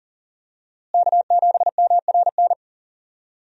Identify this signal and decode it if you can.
Morse code, heard as a keyed tone, about 39 wpm, 700 Hz. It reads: K7MRN
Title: USA Call Signs - 5 Characters - Repeated 3x - 40wpm